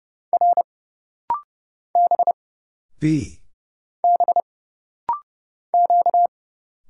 Morse code, heard as a keyed tone, about 30 wpm, 700 Hz. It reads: R B B Q
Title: Single Letters - Rapid-Fire 30wpm